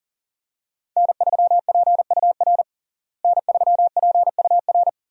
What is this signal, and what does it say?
Morse code, heard as a keyed tone, about 40 wpm, 700 Hz. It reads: N3PUR N3PUR
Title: USA Call Signs - 5 Characters - Repeated 3x - 40wpm